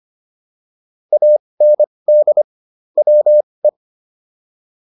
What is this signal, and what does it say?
Morse code, heard as a keyed tone, about 25 wpm, spaced as for 20 wpm, 600 Hz. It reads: AND WE